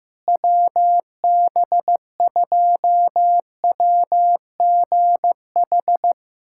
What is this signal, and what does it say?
Morse code, heard as a keyed tone, about 16 wpm, 700 Hz. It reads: WB2WGH
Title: USA Callsign DE Callsign - 6 Characters - 15wpm